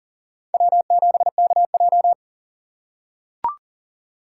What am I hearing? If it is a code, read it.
Morse code, heard as a keyed tone, about 40 wpm, 700 Hz. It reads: W7KJ
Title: USA Call Signs - 4 Characters - 6x Character Spacing - 40wpm